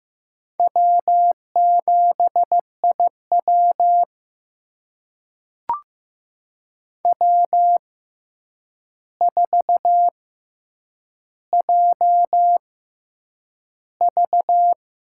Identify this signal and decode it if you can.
Morse code, heard as a keyed tone, about 15 wpm, spaced as for 8 wpm, 700 Hz. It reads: W7IW W4JV
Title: USA Call Signs - 4 Characters - 6x Character Spacing - 15wpm